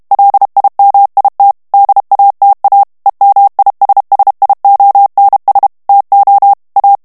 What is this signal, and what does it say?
Morse code, heard as a keyed tone, about 32 wpm, 800 Hz. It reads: LIMIT DATA EMISSIONS TO A
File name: morse mp3 / ANGLAIS texte clair 3 vitesse 30 mots minute